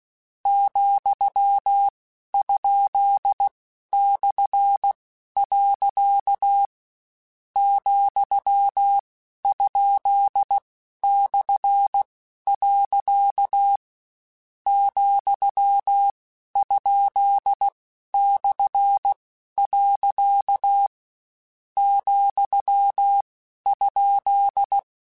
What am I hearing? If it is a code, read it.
Morse code, heard as a keyed tone, about 16 wpm, 800 Hz. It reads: , ? / . , ? / . , ? / . , ?
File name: Misc / Misc-mp3s / Punct @15